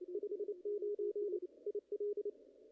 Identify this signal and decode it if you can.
Morse code, heard as a keyed tone, about 29 wpm, 410 Hz. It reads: 59 IL
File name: contests / ss-cw-k5tr-op-n5rz -2300